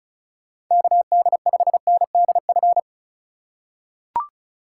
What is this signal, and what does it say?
Morse code, heard as a keyed tone, about 36 wpm, 700 Hz. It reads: KD5NDF
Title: USA Call Signs - 2x Character Spacing - 35wpm